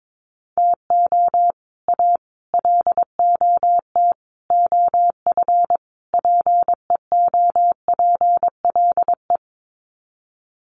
Morse code, heard as a keyed tone, about 22 wpm, 700 Hz. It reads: TO A LOT OF PEOPLE
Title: Sets of 5 Words Encore 22wpm